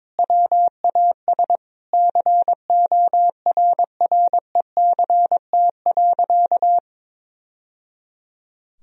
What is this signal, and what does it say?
Morse code, heard as a keyed tone, about 22 wpm, 700 Hz. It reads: WAS CORRECT.
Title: Sentences from Top 1000 Words - Review Part 2 - 22wpm